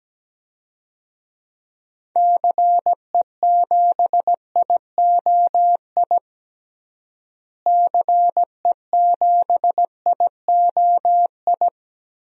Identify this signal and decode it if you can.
Morse code, heard as a keyed tone, about 17 wpm, 700 Hz. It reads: CE7IOI CE7IOI
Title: DXCC-100 Call Signs - 6 Characters - Repeated 3x - 17wpm